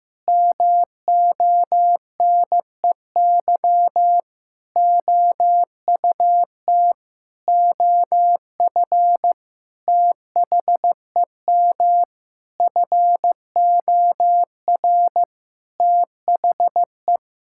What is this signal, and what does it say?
Morse code, heard as a keyed tone, about 15 wpm, 700 Hz. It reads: MONEY OUT OF THEM FOR THE
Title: Sentences from Top 1000 Words - Review Part 5 - 15wpm